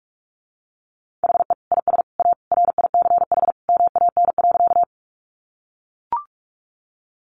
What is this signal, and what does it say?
Morse code, heard as a keyed tone, about 44 wpm, 700 Hz. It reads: HE IS A RICH MAN.
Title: Sentences from Top 1100 Words 45wpm